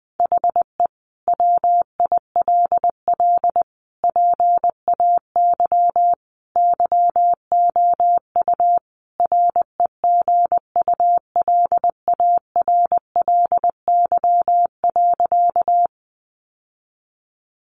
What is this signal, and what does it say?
Morse code, heard as a keyed tone, about 20 wpm, 700 Hz. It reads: HE WILL PAY YOU REGULARLY.
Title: Sentences from Top 1400 Words 20wpm